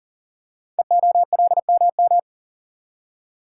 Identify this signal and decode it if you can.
Morse code, heard as a keyed tone, about 40 wpm, 700 Hz. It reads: EOLMM
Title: Code Groups - 5 Letters - 40wpm